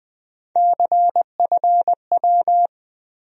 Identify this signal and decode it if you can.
Morse code, heard as a keyed tone, about 21 wpm, 700 Hz. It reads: CFW
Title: USA Call Sign Postfixes - 3 Letters 20wpm